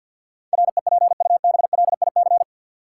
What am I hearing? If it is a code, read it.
Morse code, heard as a keyed tone, about 53 wpm, 700 Hz. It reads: REPUBLIC
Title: Sentences from Top 1500 Words - New Word Drill - 50wpm